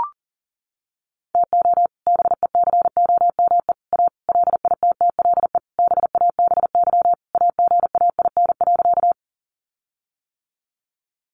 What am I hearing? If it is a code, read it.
Morse code, heard as a keyed tone, about 40 wpm, 700 Hz. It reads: TO BECOME A LITTLE BABY AGAIN.
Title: Sentences from Top 1300 Words 40wpm